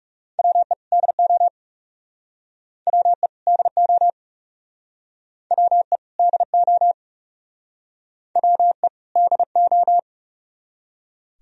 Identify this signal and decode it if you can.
Morse code, heard as a keyed tone, about 39 wpm, 700 Hz. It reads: WE DO WE DO WE DO WE DO